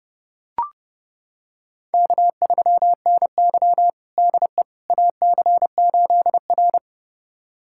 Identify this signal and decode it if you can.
Morse code, heard as a keyed tone, about 32 wpm, 700 Hz. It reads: K3NY DE AC8R
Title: USA Callsign DE Callsign - 4 Characters - 30wpm